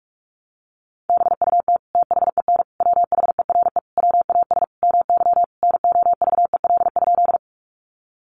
Morse code, heard as a keyed tone, about 46 wpm, 700 Hz. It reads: BUT THEN WHERE WAS MY NOVEL?
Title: Sentences from Top 1300 Words 45wpm